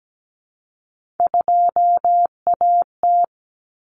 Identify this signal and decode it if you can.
Morse code, heard as a keyed tone, about 17 wpm, 700 Hz. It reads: 2AT
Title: Common Cut Numbers - 3-Digit 17wpm